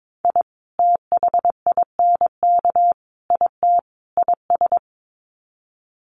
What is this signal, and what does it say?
Morse code, heard as a keyed tone, about 22 wpm, 700 Hz. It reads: I THINK IT IS
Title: Sets of 4 Words 22wpm